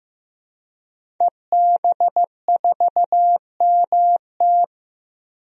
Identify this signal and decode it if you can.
Morse code, heard as a keyed tone, about 15 wpm, 700 Hz. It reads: EB4MT
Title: DXCC-100 Call Signs - 5 Characters - Repeated 3x - 15wpm